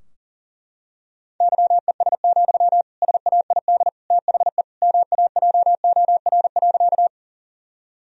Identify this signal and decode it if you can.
Morse code, heard as a keyed tone, about 40 wpm, 700 Hz. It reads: YES, SAID THE MAJOR.
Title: Sentences from Top 1000 Words - Review Part 4 - 40wpm